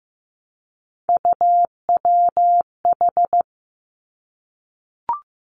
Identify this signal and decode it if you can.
Morse code, heard as a keyed tone, about 15 wpm, 700 Hz. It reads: UWH
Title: Three Challenging Letter-Number Combinations 15wpm